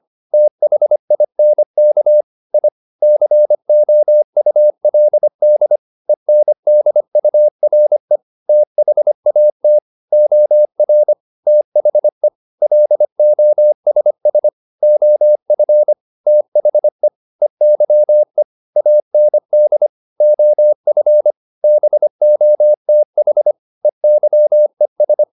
Morse code, heard as a keyed tone, about 25 wpm, 600 Hz. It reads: THINK I COULD ENDURE THAT OR THE LOSS OF THE EYE AND OF BOTH EYES
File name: the old man and the sea part 1 25wpm